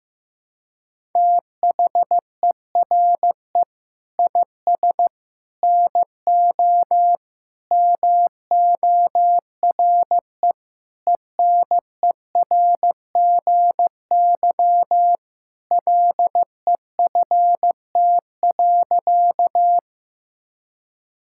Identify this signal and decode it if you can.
Morse code, heard as a keyed tone, about 15 wpm, 700 Hz. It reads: THERE IS NO MORE ENERGY LEFT.
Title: Sentences from Top 1000 Words - Review Part 6 - 15wpm